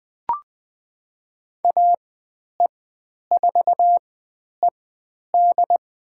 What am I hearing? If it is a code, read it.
Morse code, heard as a keyed tone, about 20 wpm, spaced as for 10 wpm, 700 Hz. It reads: AE4ED
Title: 109 Lesson 18g - Letter m - Call Signs 20wpm